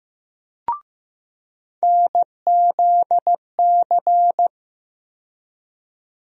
Morse code, heard as a keyed tone, about 15 wpm, 700 Hz. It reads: NZC
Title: USA Call Sign Postfixes - 3 Letters 15wpm